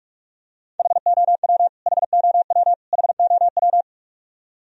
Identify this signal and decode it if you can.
Morse code, heard as a keyed tone, about 42 wpm, 700 Hz. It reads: HOW HOW HOW
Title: Top 100 Words Repeated 45wpm